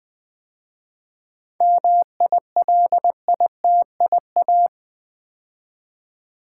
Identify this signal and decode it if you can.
Morse code, heard as a keyed tone, about 20 wpm, 700 Hz. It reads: MILITIA
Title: 7 Letter Words - First 3 Letters Given - 20wpm